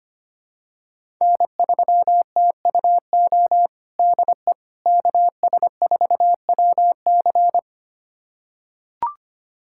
Morse code, heard as a keyed tone, about 25 wpm, 700 Hz. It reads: N3TUO DE KS4WC
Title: USA Callsign DE Callsign - 5 Characters - 25wpm